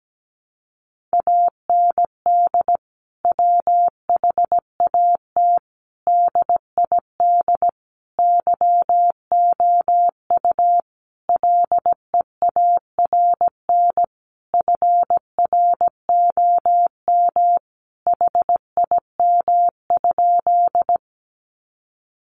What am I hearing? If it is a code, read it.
Morse code, heard as a keyed tone, about 17 wpm, 700 Hz. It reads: AND WHAT DID YOU LEARN FROM HIM?
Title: Sentences from Top 300 Words 17wpm